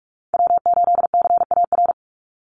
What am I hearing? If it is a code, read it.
Morse code, heard as a keyed tone, about 44 wpm, 700 Hz. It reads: W8CAR
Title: Call Signs Repeated 3x - 45wpm